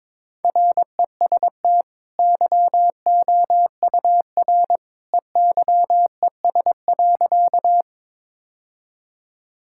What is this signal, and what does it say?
Morse code, heard as a keyed tone, about 22 wpm, 700 Hz. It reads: REST YOUR EYES.
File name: practice / 22wpm / Easy Sentences with Hints - 22wpm